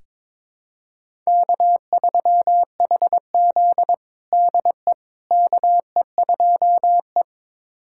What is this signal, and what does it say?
Morse code, heard as a keyed tone, about 22 wpm, 700 Hz. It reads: K3HZ DE KE2E
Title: USA Callsign DE Callsign - 4 Characters - 22wpm